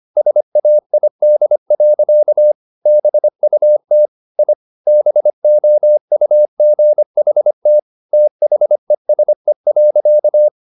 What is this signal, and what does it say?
Morse code, heard as a keyed tone, about 25 wpm, 600 Hz. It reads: SAID. BUT I BOUGHT THESE.
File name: the old man and the sea part 1 25wpm